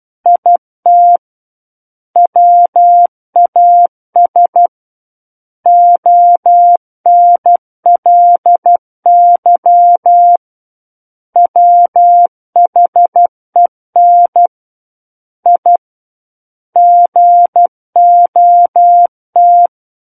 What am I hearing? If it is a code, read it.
Morse code, heard as a keyed tone, about 12 wpm, 700 Hz. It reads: IT WAS ONLY WHEN I GOT